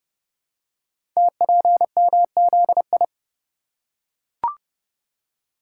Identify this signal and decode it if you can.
Morse code, heard as a keyed tone, about 30 wpm, 700 Hz. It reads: TPMZI
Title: Code Groups - 5 Letters - 30wpm